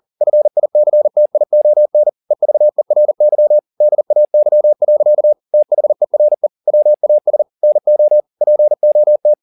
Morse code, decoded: FICTION EVERY DAY. THERE WAS NO POT

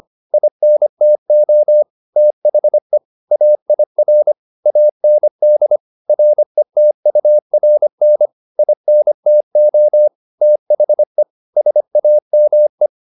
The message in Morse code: INTO THE AIR AND RETURN INTO THE SAME

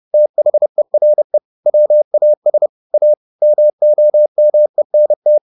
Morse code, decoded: THERE WAS A MOMENT

